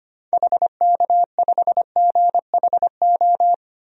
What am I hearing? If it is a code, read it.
Morse code, heard as a keyed tone, about 25 wpm, 700 Hz. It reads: HK5GHO
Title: DXCC-100 Call Signs - 6 Characters - 25wpm